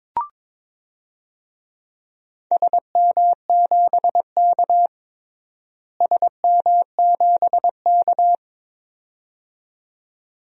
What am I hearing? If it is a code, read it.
Morse code, heard as a keyed tone, about 21 wpm, 700 Hz. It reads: SM7K SM7K
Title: DXCC-100 Call Signs - 4 Characters - Repeated 2x - 22wpm